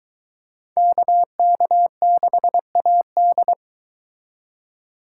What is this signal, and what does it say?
Morse code, heard as a keyed tone, about 23 wpm, 700 Hz. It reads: KK6AD